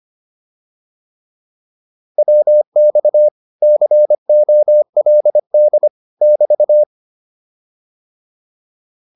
Morse code, heard as a keyed tone, about 25 wpm, 600 Hz. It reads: WX COLD <BT>
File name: QSO202 25